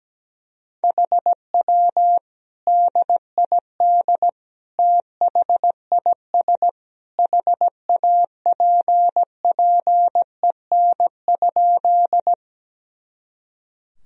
Morse code, heard as a keyed tone, about 17 wpm, 700 Hz. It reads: HW DID THIS HAPPEN?
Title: Sentences from Top 400 Words and Abbreviations 17wpm